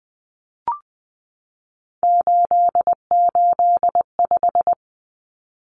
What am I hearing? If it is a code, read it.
Morse code, heard as a keyed tone, about 20 wpm, 700 Hz. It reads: 885
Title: Numbers - 3-Digit 20wpm